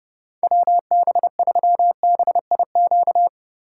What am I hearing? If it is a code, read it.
Morse code, heard as a keyed tone, about 32 wpm, 700 Hz. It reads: WB3BIQ
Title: K1USN SST QSO Practice - 30wpm